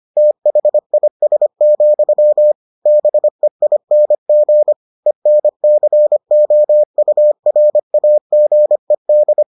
Morse code, heard as a keyed tone, about 25 wpm, 600 Hz. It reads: THIS, BEING ENCOURAGED